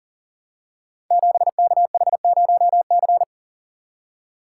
Morse code, heard as a keyed tone, about 40 wpm, 700 Hz. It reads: 7KH0C
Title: Code Groups - 5 Letter:Numbers - 40wpm